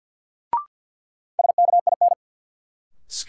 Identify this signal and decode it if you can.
Morse code, heard as a keyed tone, about 50 wpm, 700 Hz. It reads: SKIN
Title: Sentences from Top 1300 Words - New Word Drill - 50wpm